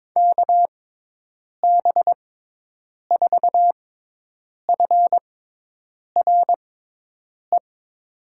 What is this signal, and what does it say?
Morse code, heard as a keyed tone, about 22 wpm, spaced as for 8 wpm, 700 Hz. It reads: KB4FRE
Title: USA Call Signs - 6x Character Spacing - 22wpm